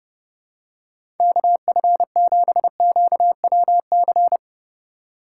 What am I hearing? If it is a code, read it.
Morse code, heard as a keyed tone, about 30 wpm, 700 Hz. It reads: KF7QWC
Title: USA Call Signs - 6 Characters - 3x Character Spacing - 30wpm